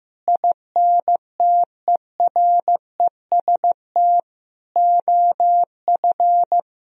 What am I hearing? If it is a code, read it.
Morse code, heard as a keyed tone, about 15 wpm, 700 Hz. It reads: INTEREST OF
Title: Sentences from Top 1000 Words - Review Part 4 - 15wpm